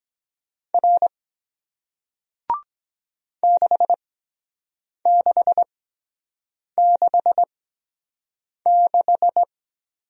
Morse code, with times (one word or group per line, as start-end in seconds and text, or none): 0.7-1.1 R
3.4-9.4 6666